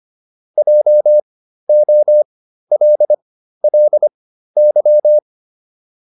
Morse code, heard as a keyed tone, about 25 wpm, spaced as for 13 wpm, 600 Hz. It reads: JOLLY
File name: LY5 13